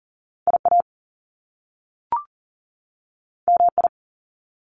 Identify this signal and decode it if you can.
Morse code, heard as a keyed tone, about 39 wpm, 700 Hz. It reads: IA MI
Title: Top Two Letter Combinations in Words 40wpm